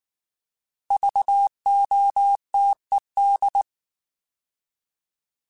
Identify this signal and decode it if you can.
Morse code, heard as a keyed tone, about 19 wpm, 800 Hz. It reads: VOTED